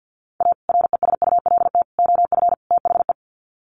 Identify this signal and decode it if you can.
Morse code, heard as a keyed tone, about 50 wpm, 700 Hz. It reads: A RESULT OF THE
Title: Sets of 4 Words 50wpm